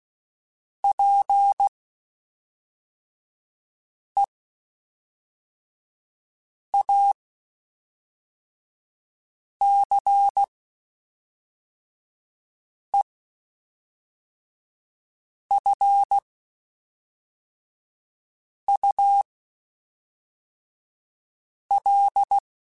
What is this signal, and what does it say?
Morse code, heard as a keyed tone, about 16 wpm, spaced as for 3 wpm, 800 Hz. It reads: PEACEFUL
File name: Quotes Spaced / Quotes Spaced-mp3s / ChildAsleep sp @5